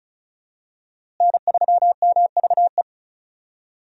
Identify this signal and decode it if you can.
Morse code, heard as a keyed tone, about 36 wpm, 700 Hz. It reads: N3MVE
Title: USA Call Signs - 5 Characters - Repeated 3x - 35wpm